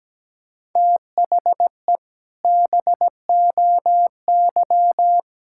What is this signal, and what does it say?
Morse code, heard as a keyed tone, about 17 wpm, 700 Hz. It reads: THE BOY
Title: Sentences from Top 1000 Words - Review Part 6 - 17wpm